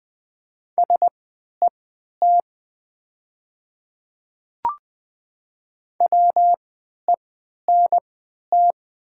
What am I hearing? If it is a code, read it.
Morse code, heard as a keyed tone, about 20 wpm, spaced as for 11 wpm, 700 Hz. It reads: SET WENT
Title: 3x Character Spacing - Top 300 Words - 20wpm